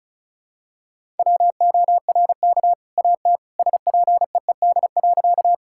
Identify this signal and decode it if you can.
Morse code, heard as a keyed tone, about 36 wpm, 700 Hz. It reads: WORK AT SPEED.
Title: Sentences from Top 1000 Words - Review Part 5 - 35wpm